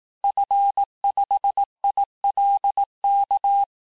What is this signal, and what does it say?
Morse code, heard as a keyed tone, about 18 wpm, 800 Hz. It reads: F5ILK
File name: cours-du-lundi files / cours-1588-FIN-de-QSO-vitesse-17MPM